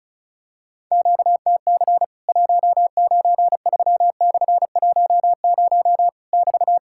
Morse code, heard as a keyed tone, about 36 wpm, 700 Hz. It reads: QTC 193/10 <BT>
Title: Worked All Europe DX Contest - QTC Exchange - 35wpm